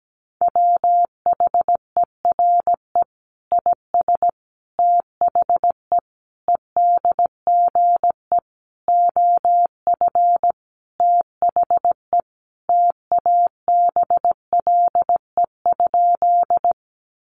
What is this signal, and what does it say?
Morse code, heard as a keyed tone, about 17 wpm, 700 Hz. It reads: WHERE IS THE EDGE OF THE TABLE?
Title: Sentences from Top 1300 Words 17wpm